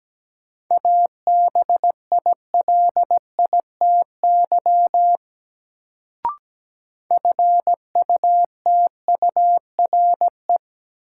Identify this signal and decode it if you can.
Morse code, heard as a keyed tone, about 17 wpm, 700 Hz. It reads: ABILITY FUTURE
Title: Sets of 3 Words - New Word Drill - 17wpm